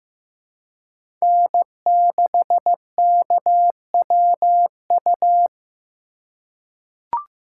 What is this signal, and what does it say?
Morse code, heard as a keyed tone, about 15 wpm, 700 Hz. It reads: N6KWU
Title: USA Call Signs - 5 Characters - 3x Character Spacing - 15wpm